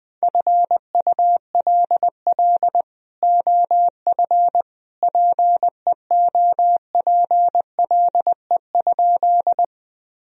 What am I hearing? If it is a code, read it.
Morse code, heard as a keyed tone, about 21 wpm, 700 Hz. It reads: FULL OF PEOPLE?
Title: Sentences from Top 1000 Words - Review Part 3 - 20wpm